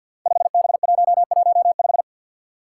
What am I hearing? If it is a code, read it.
Morse code, heard as a keyed tone, about 50 wpm, 700 Hz. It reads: 5B115
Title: Five Challenging Letter-Number Combinations 50wpm